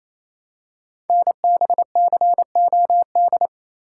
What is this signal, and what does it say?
Morse code, heard as a keyed tone, about 28 wpm, 700 Hz. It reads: NBCOD
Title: Code Groups - 5 Letters - 28wpm